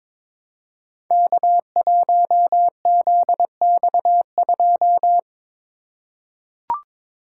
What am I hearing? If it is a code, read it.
Morse code, heard as a keyed tone, about 22 wpm, 700 Hz. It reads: K1ZX2
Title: Code Groups - 5 Letter:Numbers - 22wpm